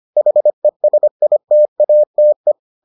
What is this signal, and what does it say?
Morse code, heard as a keyed tone, about 25 wpm, 600 Hz. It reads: HESITATE